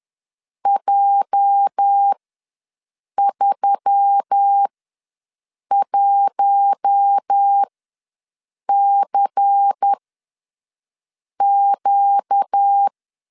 Morse code, heard as a keyed tone, about 10 wpm, spaced as for 6 wpm, 800 Hz. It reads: J31CQ
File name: LEZIONI / 8 YJ45 / 8 YJ ws 10wpm 4043